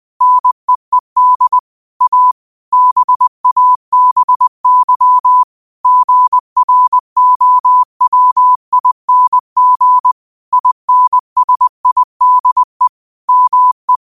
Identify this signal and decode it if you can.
Morse code, heard as a keyed tone, about 20 wpm, 1000 Hz. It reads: NEED A BABY GROWING INSIDE ME